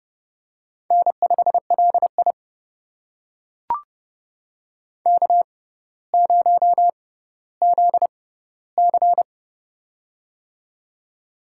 Morse code, heard as a keyed tone, about 29 wpm, spaced as for 15 wpm, 700 Hz. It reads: N5LI K0ZC
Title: USA Call Signs - 6x Character Spacing - 30wpm